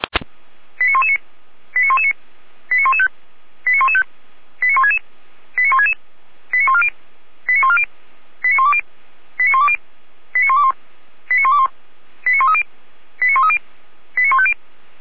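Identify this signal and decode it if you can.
Five-tone selective call (ZVEI), heard as quick runs of five tones, tones of about 70 ms. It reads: 89119 89119 89115 89115 89166 89166 89140 89140 89120 89120 89121 89121 89130 89130 89150
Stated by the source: ZVEI five-tone selective call received with an SDR, recorded by feuerwehr-orlamuende fire brigade.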